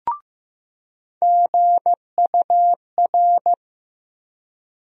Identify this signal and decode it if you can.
Morse code, heard as a keyed tone, about 15 wpm, 700 Hz. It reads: GUR
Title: USA Call Sign Postfixes - 3 Letters 15wpm